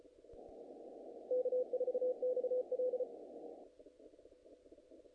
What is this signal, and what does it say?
Morse code, heard as a keyed tone, about 34 wpm, 510 Hz. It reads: K4XL
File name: contests / naqp-cw-k5tr-op-k5pi- -1800